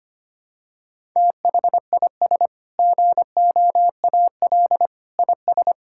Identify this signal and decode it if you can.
Morse code, heard as a keyed tone, about 25 wpm, 700 Hz. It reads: THIS GOAL IS
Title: Sentences from Top 1000 Words - Review Part 5 - 25wpm